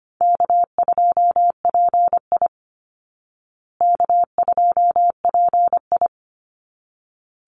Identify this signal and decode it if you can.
Morse code, heard as a keyed tone, about 25 wpm, 700 Hz. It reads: K2PI K2PI
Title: Easy Call Signs Repeated 2x - 25wpm